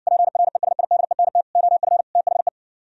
Morse code, heard as a keyed tone, about 60 wpm, 700 Hz. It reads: PRESIDENT OF THE